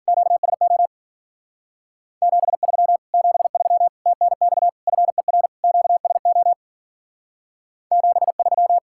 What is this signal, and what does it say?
Morse code, heard as a keyed tone, about 49 wpm, 700 Hz. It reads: QSO 73 73 TNX FER QSO 73
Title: Speed-Racing Common QSO Phrases - 40wpm